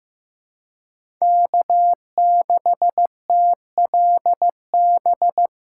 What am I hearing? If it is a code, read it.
Morse code, heard as a keyed tone, about 15 wpm, 700 Hz. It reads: K6TLB
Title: USA Call Signs - 5 Characters - 2x Character Spacing - 15wpm